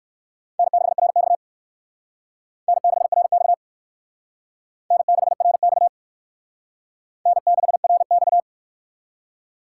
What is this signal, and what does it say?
Morse code, heard as a keyed tone, about 56 wpm, 700 Hz. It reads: N6RX N6RX N6RX N6RX
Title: Speed Racing - USA Call Signs 45wpm